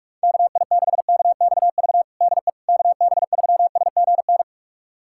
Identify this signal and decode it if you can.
Morse code, heard as a keyed tone, about 42 wpm, 700 Hz. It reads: KI6KXV DE KB3SGN
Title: USA Callsign DE Callsign 45wpm